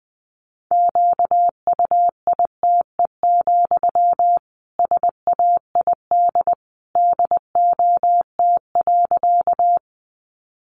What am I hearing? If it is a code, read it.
Morse code, heard as a keyed tone, about 19 wpm, 700 Hz. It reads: QUITE, SAID DOT.
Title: Sentences from Top 1500 Words 20wpm